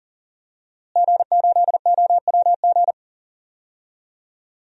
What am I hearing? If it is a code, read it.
Morse code, heard as a keyed tone, about 40 wpm, 700 Hz. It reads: G8OWG
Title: DXCC-100 Call Signs - 5 Characters - Repeated 3x - 40wpm